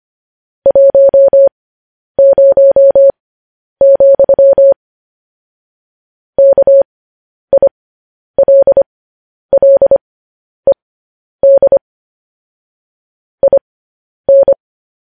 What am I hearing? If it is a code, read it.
Morse code, heard as a keyed tone, about 25 wpm, spaced as for 10 wpm, 550 Hz. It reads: 10, KILLED IN
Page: news headlines in morse code at 10 wpm